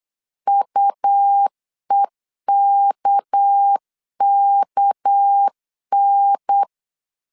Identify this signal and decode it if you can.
Morse code, heard as a keyed tone, about 8 wpm, 800 Hz. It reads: UEKKN
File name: LEZIONI / 6 PKX0 / 6 PK 8wpm 3003